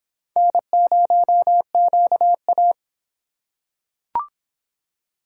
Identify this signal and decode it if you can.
Morse code, heard as a keyed tone, about 25 wpm, 700 Hz. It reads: N0QA